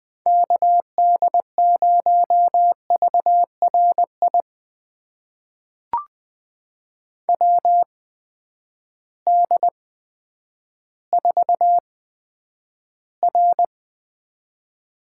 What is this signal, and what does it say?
Morse code, heard as a keyed tone, about 20 wpm, spaced as for 10 wpm, 700 Hz. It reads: KD0VRI WD4R
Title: USA Call Signs - 6 Characters - 8x Character Spacing - 20wpm